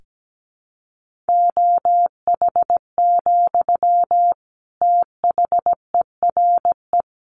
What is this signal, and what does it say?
Morse code, heard as a keyed tone, about 17 wpm, 700 Hz. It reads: OH, THERE
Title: Sentences from Top 1500 Words 17wpm